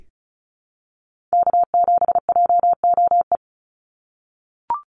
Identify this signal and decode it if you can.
Morse code, heard as a keyed tone, about 34 wpm, 700 Hz. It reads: K7JOE
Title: Call Signs - 35wpm